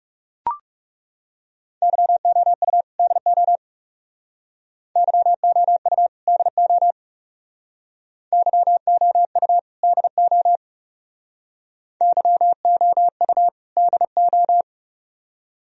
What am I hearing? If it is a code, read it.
Morse code, heard as a keyed tone, about 38 wpm, 700 Hz. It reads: YOU DO YOU DO YOU DO YOU DO